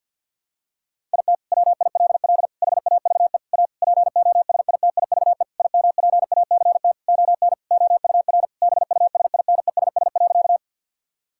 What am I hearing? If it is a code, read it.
Morse code, heard as a keyed tone, about 50 wpm, 700 Hz. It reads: IT WILL HAVE A POSITIVE IMPACT ON OUR BUSINESS.
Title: Sentences from Top 1000 Words - Review Part 6 - 50wpm